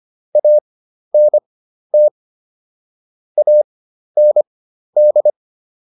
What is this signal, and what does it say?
Morse code, heard as a keyed tone, about 25 wpm, spaced as for 12 wpm, 600 Hz. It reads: ANT AND